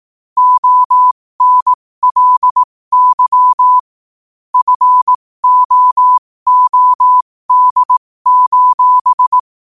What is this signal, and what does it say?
Morse code, heard as a keyed tone, about 18 wpm, spaced as for 15 wpm, 1000 Hz. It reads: ONLY FOOD:
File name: qotdfiles / 15wpm